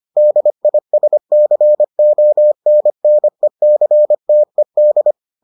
Morse code, heard as a keyed tone, about 25 wpm, 600 Hz. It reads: DISCONNECTED